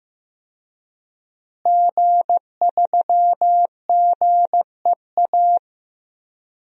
Morse code, heard as a keyed tone, about 15 wpm, 700 Hz. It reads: G3GEA
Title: DXCC-100 Call Signs - 5 Characters - Repeated 3x - 15wpm